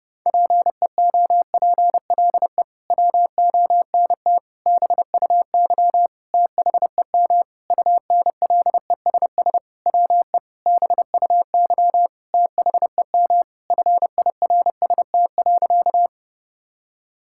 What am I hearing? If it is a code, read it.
Morse code, heard as a keyed tone, about 28 wpm, 700 Hz. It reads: PEOPLE WONT BUY THEM UNLESS WE BUY THEM FIRST.
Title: Sentences from Top 1000 Words - Review Part 6 - 30wpm